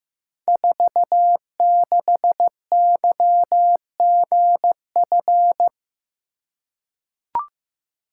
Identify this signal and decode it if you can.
Morse code, heard as a keyed tone, about 15 wpm, 700 Hz. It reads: 46YGF
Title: Code Groups - 5 Letter:Numbers - 15wpm